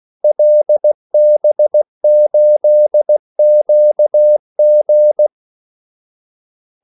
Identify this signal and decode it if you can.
Morse code, heard as a keyed tone, about 16 wpm, 600 Hz. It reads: LB8QG
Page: Kaldesignaler 16 wpm